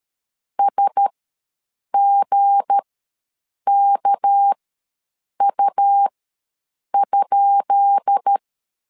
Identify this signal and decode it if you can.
Morse code, heard as a keyed tone, about 13 wpm, spaced as for 7 wpm, 800 Hz. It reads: SGKU?
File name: LEZIONI / 10 SIGNS / 10B INT ws 12wpm 3527